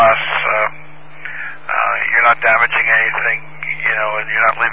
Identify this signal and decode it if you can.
Emergency Alert System (SAME) tones, heard as harsh buzzing bursts, with speech. It is heard from 1.1 to 4.0 s.